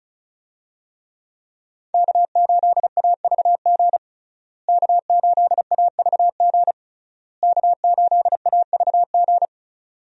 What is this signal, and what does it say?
Morse code, heard as a keyed tone, about 34 wpm, 700 Hz. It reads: K8AVG K8AVG K8AVG
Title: USA Call Signs - 5 Characters - Repeated 3x - 35wpm